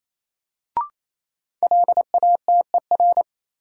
Morse code, heard as a keyed tone, about 28 wpm, 700 Hz. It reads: LATER
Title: Sentences from Top 700 Words - New Word Drill - 28wpm